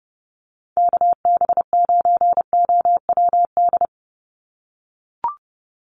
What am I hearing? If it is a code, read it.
Morse code, heard as a keyed tone, about 30 wpm, 700 Hz. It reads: KB9OWD
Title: Call Signs - 30wpm